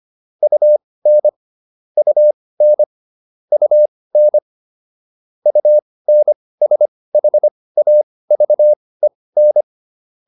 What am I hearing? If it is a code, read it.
Morse code, heard as a keyed tone, about 25 wpm, spaced as for 18 wpm, 600 Hz. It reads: UN UN UN UNSHAVEN